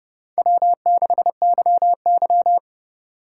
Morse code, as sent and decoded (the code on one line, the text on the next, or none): .-- -.... -.-- -.--
W6YY